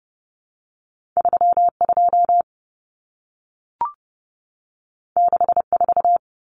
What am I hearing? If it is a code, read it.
Morse code, heard as a keyed tone, about 30 wpm, 700 Hz. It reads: 32 64
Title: Common Cut Numbers - 2-Digit 30wpm